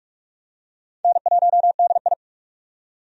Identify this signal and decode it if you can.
Morse code, heard as a keyed tone, about 45 wpm, 700 Hz. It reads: N1DI
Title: USA Call Signs - 4 Characters - 45wpm